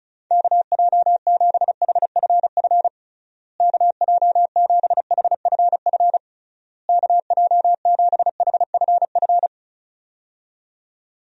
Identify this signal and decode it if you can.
Morse code, heard as a keyed tone, about 36 wpm, 700 Hz. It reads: KJ7HFF KJ7HFF KJ7HFF
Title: USA Call Signs - 6 Characters - Repeated 3x - 35wpm